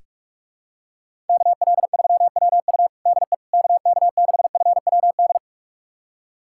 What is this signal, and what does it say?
Morse code, heard as a keyed tone, about 43 wpm, 700 Hz. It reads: KL3WU DE KK6FWD
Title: USA Callsign DE Callsign 45wpm